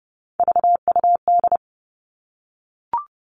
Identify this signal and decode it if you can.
Morse code, heard as a keyed tone, about 30 wpm, 700 Hz. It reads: VUD
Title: All Cut Numbers - 3-Digit 30wpm